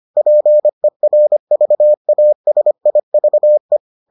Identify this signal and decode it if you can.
Morse code, heard as a keyed tone, about 25 wpm, 600 Hz. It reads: PERVASIVE